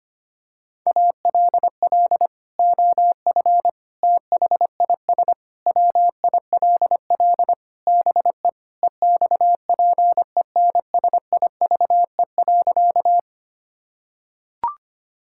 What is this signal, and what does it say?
Morse code, heard as a keyed tone, about 25 wpm, 700 Hz. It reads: ALL OF THIS WILL BE EXPENSIVE.
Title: Sentences from Top 1000 Words - Review Part 6 - 25wpm